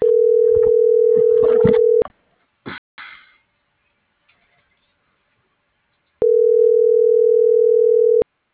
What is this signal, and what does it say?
Ringback tone (North America), 440 + 480 Hz.